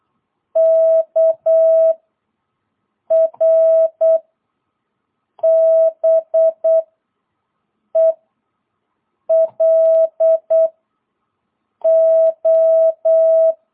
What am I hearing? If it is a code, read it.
Morse code, heard as a keyed tone, about 8 wpm, 660 Hz. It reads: K R B E L O
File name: jh8cbh / 2022cw